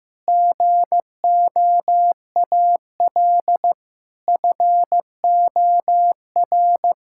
Morse code, heard as a keyed tone, about 15 wpm, 700 Hz. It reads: GOAL FOR
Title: Sentences from Top 1000 Words - Review Part 5 - 15wpm